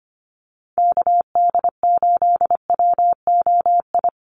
Morse code, heard as a keyed tone, about 25 wpm, 700 Hz. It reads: KD8WOI
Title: USA Call Signs Repeated 2x 25wpm